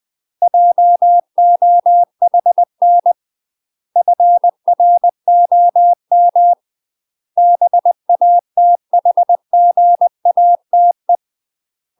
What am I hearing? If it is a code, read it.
Morse code, heard as a keyed tone, about 20 wpm, 700 Hz. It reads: JOHN FROM BATHGATE
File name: UK MASTER QSOs TRAINING / 20WPM / UK MASTER QSOs 0040